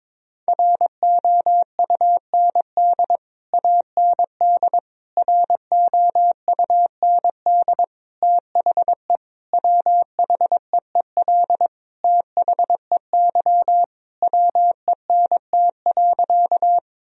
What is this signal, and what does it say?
Morse code, heard as a keyed tone, about 22 wpm, 700 Hz. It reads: ROUND AND ROUND THE WHEEL THEY WENT.
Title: Sentences from Top 1000 Words - Review Part 3 - 22wpm